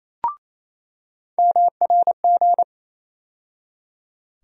Morse code, heard as a keyed tone, about 28 wpm, 700 Hz. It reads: MRG
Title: USA Call Sign Postfixes - 3 Letters 28wpm